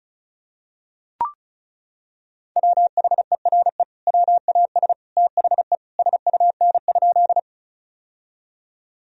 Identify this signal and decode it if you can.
Morse code, heard as a keyed tone, about 36 wpm, 700 Hz. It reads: WHERE WAS THE SUN?
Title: Sentences from Top 1000 Words - Review Part 1 - 35wpm